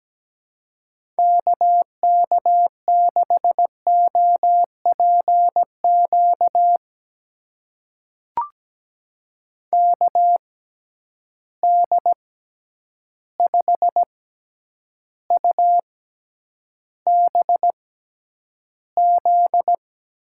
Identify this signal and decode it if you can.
Morse code, heard as a keyed tone, about 17 wpm, spaced as for 9 wpm, 700 Hz. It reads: KK6OPQ KD5UBZ
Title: USA Call Signs - 6 Characters - 6x Character Spacing - 17wpm